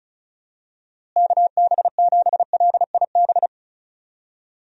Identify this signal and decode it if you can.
Morse code, heard as a keyed tone, about 34 wpm, 700 Hz. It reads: KB7LIB
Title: USA Call Signs - 6 Characters - Repeated 2x - 35wpm